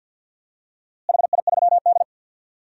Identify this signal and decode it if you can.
Morse code, heard as a keyed tone, about 51 wpm, 700 Hz. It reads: HI3D
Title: DXCC-100 Call Signs - 4 Characters - Repeated 3x - 50wpm